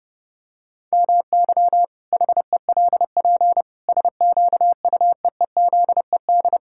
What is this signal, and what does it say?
Morse code, heard as a keyed tone, about 28 wpm, 700 Hz. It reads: MY HELP SQUEEZED
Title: Pangram Challenge v2 - 30wpm